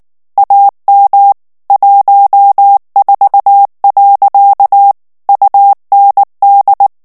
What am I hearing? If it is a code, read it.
Morse code, heard as a keyed tone, about 19 wpm, 800 Hz. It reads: AM 14. UND
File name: morse mp3 / ALLEMAND texte clair 2 vitesse 18 mots minute